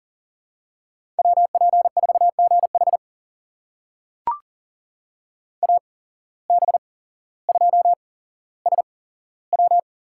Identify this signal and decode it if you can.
Morse code, heard as a keyed tone, about 40 wpm, spaced as for 17 wpm, 700 Hz. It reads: WP4GH AB2SW
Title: USA Call Signs - 5 Characters - 8x Character Spacing - 40wpm